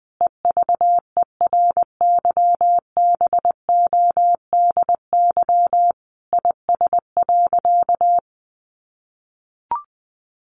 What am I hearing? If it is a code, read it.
Morse code, heard as a keyed tone, about 20 wpm, 700 Hz. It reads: EVERYBODY IS.
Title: Sentences from Top 1400 Words 20wpm